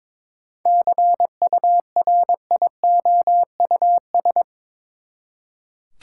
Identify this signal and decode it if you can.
Morse code, heard as a keyed tone, about 22 wpm, 700 Hz. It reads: CURIOUS
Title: Seven Letter Words - 22wpm